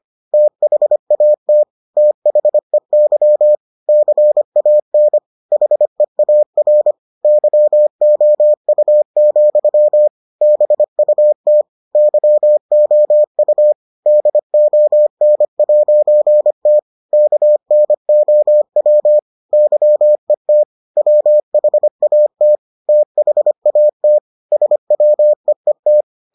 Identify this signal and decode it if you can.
Morse code, heard as a keyed tone, about 25 wpm, 600 Hz. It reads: THAT THEY CAN HEAR YOU, BUT YOU DON'T KNOW YET WHAT THAT SWEET